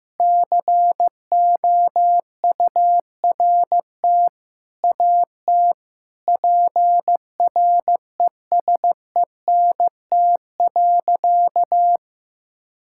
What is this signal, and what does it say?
Morse code, heard as a keyed tone, about 15 wpm, 700 Hz. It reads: COURT AT PRESENT.
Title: Sentences from Top 1600 Words 15wpm